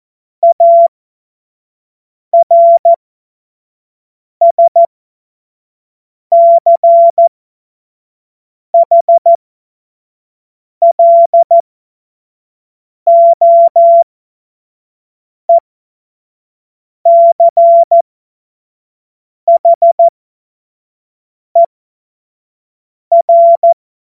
Morse code, heard as a keyed tone, about 14 wpm, spaced as for 5 wpm, 680 Hz. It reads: ARSCHLOECHER
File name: Morsekurs / Witze / Witz 04. April 2023 25 F70